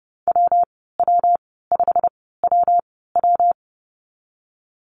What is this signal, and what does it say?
Morse code, heard as a keyed tone, about 30 wpm, spaced as for 17 wpm, 700 Hz. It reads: WW5WW
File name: course / 30:10wpm / 116 Lesson 19g - Letter w - Call Signs 30wpm